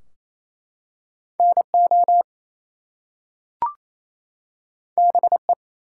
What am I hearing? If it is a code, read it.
Morse code, heard as a keyed tone, about 27 wpm, 700 Hz. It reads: NO BE
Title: Two Letter Words - 28wpm